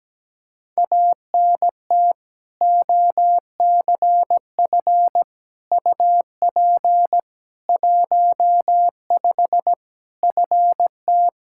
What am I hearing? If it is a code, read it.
Morse code, heard as a keyed tone, about 17 wpm, 700 Hz. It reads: ANT OCF UP 15 FT
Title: Practice QSO - Exchange 2 - Info - 17wpm